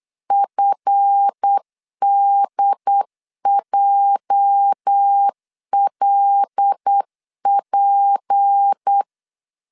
Morse code, heard as a keyed tone, about 8 wpm, 800 Hz. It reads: FDJLP
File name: LEZIONI / 8 YJ45 / 8 YJ 8wpm 3313